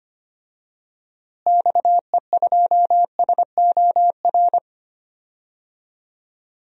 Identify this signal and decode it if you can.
Morse code, heard as a keyed tone, about 25 wpm, 700 Hz. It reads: XE2SOR
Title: DXCC-100 Call Signs - 6 Characters - Repeated 2x - 25wpm